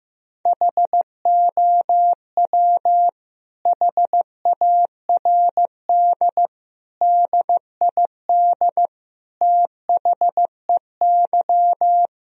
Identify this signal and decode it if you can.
Morse code, heard as a keyed tone, about 15 wpm, 700 Hz. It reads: HOW HARD DID THEY
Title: Sentences from Top 1000 Words - Review Part 4 - 15wpm